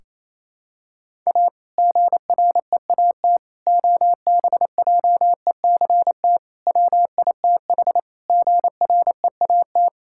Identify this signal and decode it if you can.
Morse code, heard as a keyed tone, about 28 wpm, 700 Hz. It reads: A GREAT OBJECT WITH GREAT
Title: Sentences from Top 1600 Words 28wpm